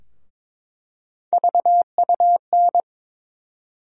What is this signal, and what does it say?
Morse code, heard as a keyed tone, about 22 wpm, 700 Hz. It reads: VUN
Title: USA Call Sign Postfixes - 3 Letters 22wpm